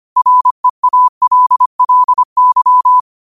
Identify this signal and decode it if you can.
Morse code, heard as a keyed tone, about 25 wpm, 1000 Hz. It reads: REALLY
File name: qotdfiles / 25wpm